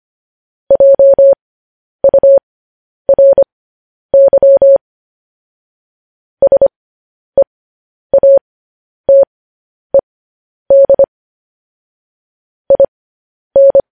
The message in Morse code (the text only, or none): JURY SEATED IN